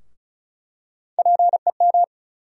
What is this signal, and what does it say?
Morse code, heard as a keyed tone, about 36 wpm, 700 Hz. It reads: PEM
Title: USA Call Sign Postfixes - 3 Letters 35wpm